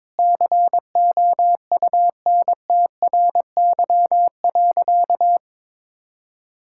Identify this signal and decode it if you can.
Morse code, heard as a keyed tone, about 22 wpm, 700 Hz. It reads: COUNTRY.
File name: Sentences with 9-Word Phrases - 22wpm